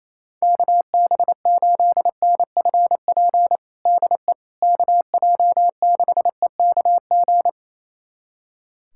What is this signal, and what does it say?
Morse code, heard as a keyed tone, about 28 wpm, 700 Hz. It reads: KB8NFP DE KJ6EKG
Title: USA Callsign DE Callsign - 6 Characters - 28wpm